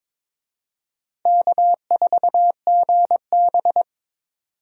Morse code, heard as a keyed tone, about 22 wpm, 700 Hz. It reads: K4GB
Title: USA Call Signs - 4 Characters - 22wpm